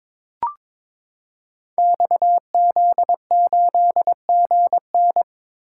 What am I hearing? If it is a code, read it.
Morse code, heard as a keyed tone, about 22 wpm, 700 Hz. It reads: XZ8GN
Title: Code Groups - 5 Letter:Numbers - 22wpm